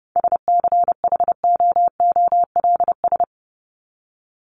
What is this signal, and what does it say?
Morse code, heard as a keyed tone, about 30 wpm, 700 Hz. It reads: SCHOOLS
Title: ES:S Suffix - Top 100 Words with Base Words 30wpm